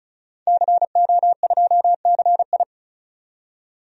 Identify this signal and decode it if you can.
Morse code, heard as a keyed tone, about 36 wpm, 700 Hz. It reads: CO2CI
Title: DXCC-100 Call Signs - 5 Characters - 35wpm